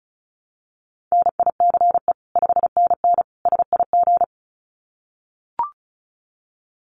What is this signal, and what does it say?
Morse code, heard as a keyed tone, about 34 wpm, 700 Hz. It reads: NICE 5NN SIG
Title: Common QSO Phrases 35wpm